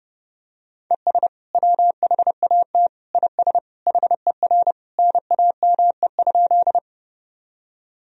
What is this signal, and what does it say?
Morse code, heard as a keyed tone, about 30 wpm, 700 Hz. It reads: ES WHAT IS HER NAME?
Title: Sentences from Top 200 Words and Abbreviations 30wpm